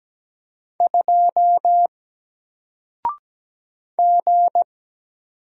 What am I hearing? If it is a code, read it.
Morse code, heard as a keyed tone, about 17 wpm, 700 Hz. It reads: 2 G
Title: Single Letter-Number Alt 17wpm